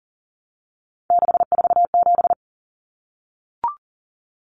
Morse code, heard as a keyed tone, about 40 wpm, 700 Hz. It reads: 647
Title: Numbers - 3-Digit 40wpm